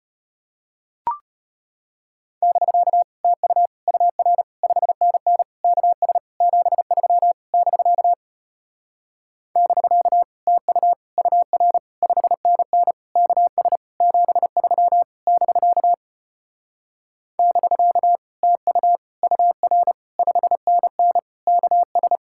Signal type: Morse code

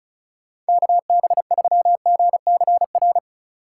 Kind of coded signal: Morse code

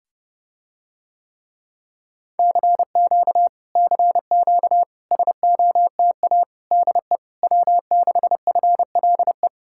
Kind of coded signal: Morse code

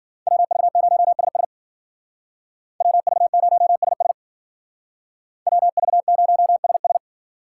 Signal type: Morse code

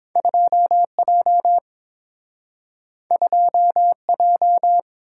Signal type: Morse code

